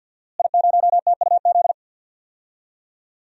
Morse code, heard as a keyed tone, about 51 wpm, 700 Hz. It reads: I0TUZ
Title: DXCC-100 Call Signs - 5 Characters - Repeated 3x - 50wpm